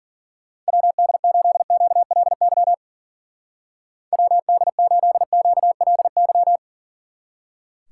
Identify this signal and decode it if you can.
Morse code, heard as a keyed tone, about 50 wpm, 700 Hz. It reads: WD8QLY WD8QLY